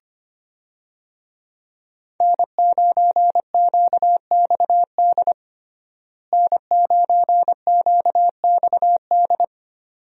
Morse code, heard as a keyed tone, about 25 wpm, 700 Hz. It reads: N9QXD N9QXD
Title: USA Call Signs - 5 Characters - Repeated 2x - 25wpm